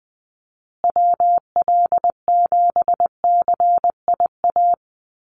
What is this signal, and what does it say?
Morse code, heard as a keyed tone, about 20 wpm, 700 Hz. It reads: WL7CIA
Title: USA Call Signs Repeated 2x 20wpm